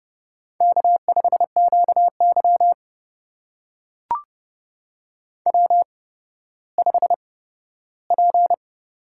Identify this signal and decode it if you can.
Morse code, heard as a keyed tone, about 30 wpm, spaced as for 15 wpm, 700 Hz. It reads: K5QY W5P
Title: USA Call Signs - 4 Characters - 8x Character Spacing - 30wpm